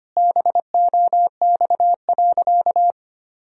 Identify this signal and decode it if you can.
Morse code, heard as a keyed tone, about 25 wpm, 700 Hz. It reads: BOX.
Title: Sentences from Top 1000 Words - Review Part 2 - 25wpm